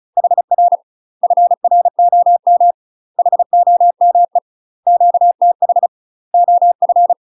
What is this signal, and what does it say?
Morse code, heard as a keyed tone, about 36 wpm, 700 Hz. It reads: HR FROM HOME QTH OF
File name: UK MASTER QSOs TRAINING / 35WPM / UK MASTER QSOs 0029